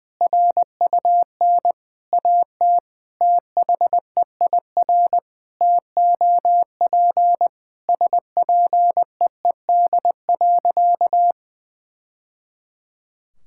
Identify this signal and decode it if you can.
Morse code, heard as a keyed tone, about 20 wpm, 700 Hz. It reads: RUN AT THEIR TOP SPEED.
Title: Sentences from Top 1000 Words - Review Part 5 - 20wpm